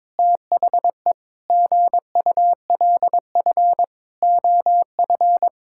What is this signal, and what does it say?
Morse code, heard as a keyed tone, about 22 wpm, 700 Hz. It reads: THE GULF OF
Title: Pangram Challenge v2 - 22wpm